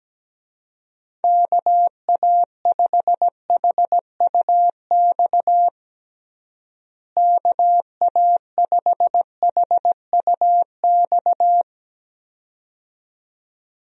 Morse code, heard as a keyed tone, about 17 wpm, 700 Hz. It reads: KA5HUX KA5HUX
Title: USA Call Signs - 6 Characters - Repeated 2x - 17wpm